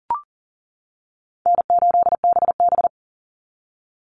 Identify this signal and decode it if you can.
Morse code, heard as a keyed tone, about 40 wpm, 700 Hz. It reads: N8BB
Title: NA SOTA Chaser Callsigns 40wpm